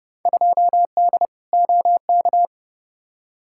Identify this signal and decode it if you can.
Morse code, heard as a keyed tone, about 30 wpm, 700 Hz. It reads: 2D OK
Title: Field Day Exchanges Encore - 30wpm